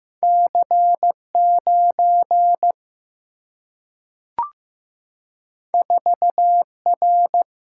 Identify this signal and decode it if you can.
Morse code, heard as a keyed tone, about 15 wpm, 700 Hz. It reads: C9 4R
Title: Single-Letter Single-Number 15wpm